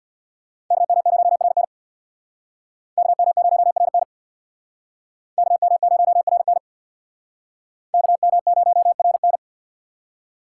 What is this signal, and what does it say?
Morse code, heard as a keyed tone, about 61 wpm, 700 Hz. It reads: KM0RN KM0RN KM0RN KM0RN
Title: Speed Racing - USA 2x2 Call Signs 50wpm